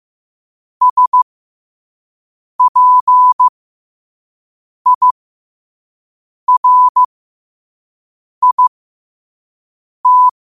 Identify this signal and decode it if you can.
Morse code, heard as a keyed tone, about 15 wpm, spaced as for 5 wpm, 1000 Hz. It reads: SPIRIT
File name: qotdfiles / 5wpm